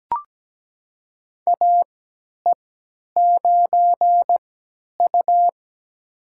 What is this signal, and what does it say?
Morse code, heard as a keyed tone, about 17 wpm, spaced as for 10 wpm, 700 Hz. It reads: AE9U
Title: USA Call Signs - 4 Characters - 3x Character Spacing - 17wpm